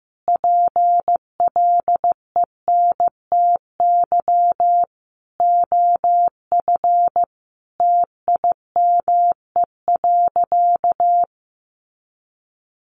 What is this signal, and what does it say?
Morse code, heard as a keyed tone, about 15 wpm, 700 Hz. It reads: PLENTY OF TIME.
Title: Sentences from Top 1300 Words 15wpm